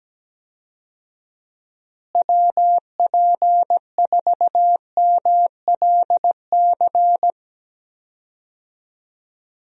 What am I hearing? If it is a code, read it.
Morse code, heard as a keyed tone, about 17 wpm, 700 Hz. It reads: WP4MLC
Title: USA Call Signs - 6 Characters - Repeated 2x - 17wpm